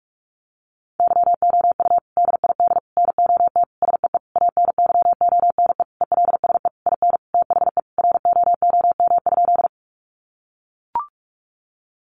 Morse code, heard as a keyed tone, about 44 wpm, 700 Hz. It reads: YOU DID NOT SEE ANYONE ELSE IN THE ROOM?
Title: Sentences from Top 900 Words 45wpm